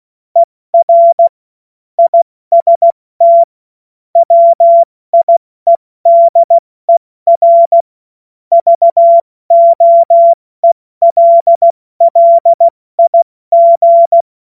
Morse code, heard as a keyed tone, about 16 wpm, 680 Hz. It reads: ER IST WIEDER VOELLIG
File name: Morsekurs / Witze / Witz 28. Dezember 2021, 70 F80